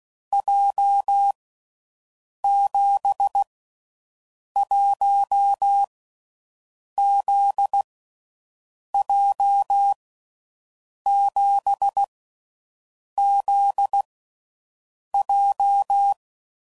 Morse code, heard as a keyed tone, about 16 wpm, 800 Hz. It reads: J 7 1 Z J 7 Z J
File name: Groups / Basic Letter Groups / JZ17